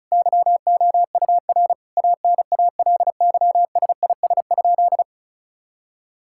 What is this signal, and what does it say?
Morse code, heard as a keyed tone, about 36 wpm, 700 Hz. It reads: YOUR ANALYSIS?
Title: Sentences from Top 1000 Words - Review Part 5 - 35wpm